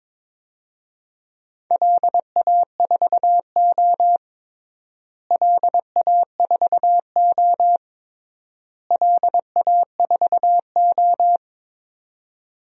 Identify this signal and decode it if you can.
Morse code, heard as a keyed tone, about 22 wpm, 700 Hz. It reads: LA4O LA4O LA4O
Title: DXCC-100 Call Signs - 4 Characters - Repeated 3x - 22wpm